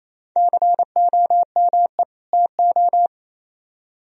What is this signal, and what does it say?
Morse code, heard as a keyed tone, about 38 wpm, spaced as for 33 wpm, 700 Hz. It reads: COME TO
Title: Speed-Racing - Sets of 2 Words 28wpm